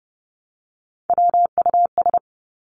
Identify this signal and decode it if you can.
Morse code, heard as a keyed tone, about 30 wpm, 700 Hz. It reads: WUS
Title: Three Challenging Letter-Number Combinations 30wpm